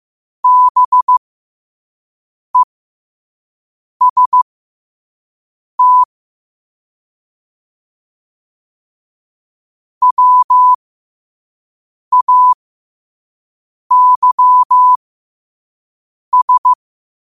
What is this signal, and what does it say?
Morse code, heard as a keyed tone, about 15 wpm, spaced as for 5 wpm, 1000 Hz. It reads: BEST WAYS
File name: qotdfiles / 5wpm